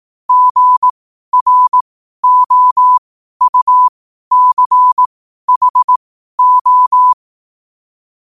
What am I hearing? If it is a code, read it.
Morse code, heard as a keyed tone, about 18 wpm, spaced as for 12 wpm, 1000 Hz. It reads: GROUCHO
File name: qotdfiles / 12wpm